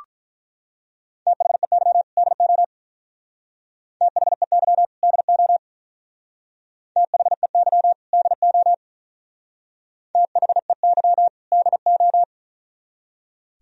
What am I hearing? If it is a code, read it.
Morse code, heard as a keyed tone, about 45 wpm, 700 Hz. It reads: THEY DO THEY DO THEY DO THEY DO